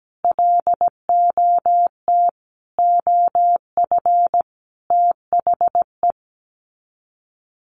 Morse code, heard as a keyed tone, about 17 wpm, 700 Hz. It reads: LOT OF THE
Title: Sets of 5 Words Encore 17wpm